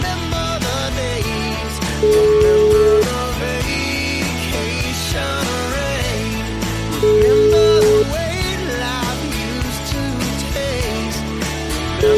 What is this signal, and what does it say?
Ringback tone (Europe, 425 Hz), with noise.